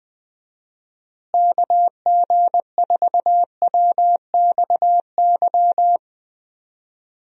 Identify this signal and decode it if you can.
Morse code, heard as a keyed tone, about 20 wpm, 700 Hz. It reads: KG4WXY
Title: USA Call Signs - 6 Characters - Repeated 3x - 20wpm